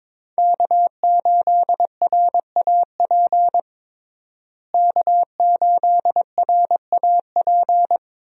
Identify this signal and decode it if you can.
Morse code, heard as a keyed tone, about 22 wpm, 700 Hz. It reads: K8RAP K8RAP
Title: USA Call Signs - 5 Characters - Repeated 3x - 22wpm